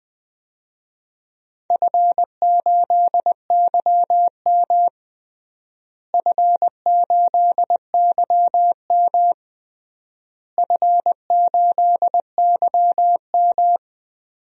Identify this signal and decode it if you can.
Morse code, heard as a keyed tone, about 20 wpm, 700 Hz. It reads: F8YM F8YM F8YM
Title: DXCC-100 Call Signs - 4 Characters - Repeated 3x - 20wpm